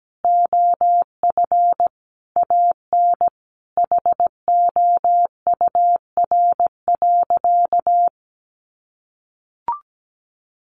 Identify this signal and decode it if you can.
Morse code, heard as a keyed tone, about 17 wpm, 700 Hz. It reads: OF AN HOUR.
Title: Sentences from Top 1300 Words 17wpm